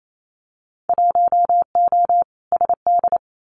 Morse code, heard as a keyed tone, about 28 wpm, 700 Hz. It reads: 1O SD
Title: Winter Field Day Exchanges Encore 28wpm